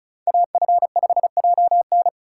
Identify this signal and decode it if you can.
Morse code, heard as a keyed tone, about 36 wpm, 700 Hz. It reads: AF5JN
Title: USA Call Signs - 5 Characters - 2x Character Spacing - 35wpm